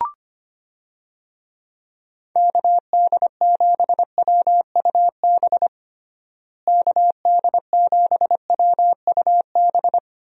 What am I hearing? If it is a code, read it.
Morse code, heard as a keyed tone, about 25 wpm, 700 Hz. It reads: KD7WUB KD7WUB
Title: USA Call Signs - 6 Characters - Repeated 3x - 25wpm